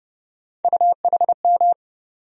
Morse code, heard as a keyed tone, about 31 wpm, 700 Hz. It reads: UHM
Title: USA Call Sign Postfixes - 3 Letters 30wpm